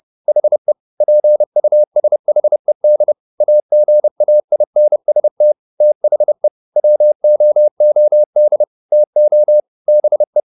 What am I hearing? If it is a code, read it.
Morse code, heard as a keyed tone, about 30 wpm, 600 Hz. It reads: HE PUSHED AGAINST THE WOOD TO BE
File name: the old man and the sea / the old man and the sea 1 30wpm